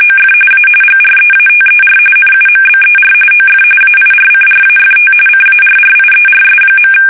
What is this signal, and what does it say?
RTTY, about 45.45 baud. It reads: OF A GREATER NUMBER OF AIRCRAFT CARRIERS.
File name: fsk850 / battleships